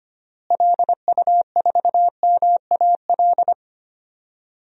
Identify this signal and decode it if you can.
Morse code, heard as a keyed tone, about 25 wpm, 700 Hz. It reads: LU4MAL
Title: DXCC-100 Call Signs - 6 Characters - Repeated 3x - 25wpm